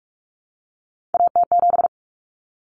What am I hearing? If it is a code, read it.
Morse code, heard as a keyed tone, about 44 wpm, 700 Hz. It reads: AT7
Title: Common Cut Numbers - 3-Digit 45wpm